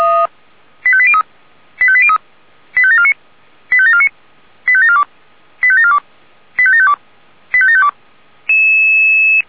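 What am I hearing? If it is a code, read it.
Five-tone selective call (ZVEI), heard as quick runs of five tones, tones of about 70 ms. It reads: 85803 85803 85749 85749 85742 85742 85752 85752
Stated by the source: ZVEI five-tone selective call received by SDR.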